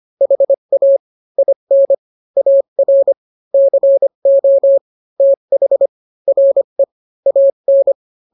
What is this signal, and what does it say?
Morse code, heard as a keyed tone, about 25 wpm, 550 Hz. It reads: HA IN AR CO TH RE AN